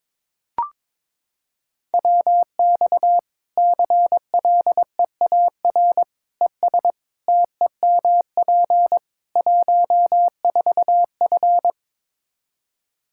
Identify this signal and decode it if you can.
Morse code, heard as a keyed tone, about 22 wpm, 700 Hz. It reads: WX CLEAR ES TEMP 14F
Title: K1USN SST QSO Practice 22wpm